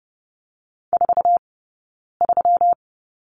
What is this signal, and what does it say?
Morse code, heard as a keyed tone, about 30 wpm, 700 Hz. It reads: 4 3
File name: Lesson 20d - Number 3 - Two Characters 30wpm